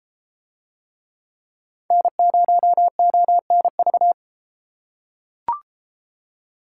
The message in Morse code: N0ONV

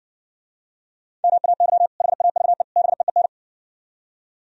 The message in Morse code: MAY HAVE BEEN

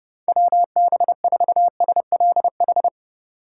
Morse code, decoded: WB4SLH

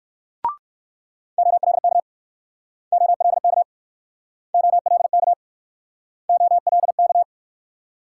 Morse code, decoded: OLK OLK OLK OLK

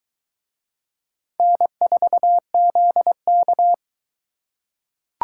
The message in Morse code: N4ZK

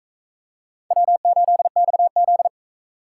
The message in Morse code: W8XZ

